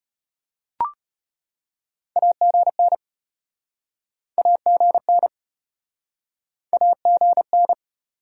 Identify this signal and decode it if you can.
Morse code, heard as a keyed tone, about 36 wpm, 700 Hz. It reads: AGN AGN AGN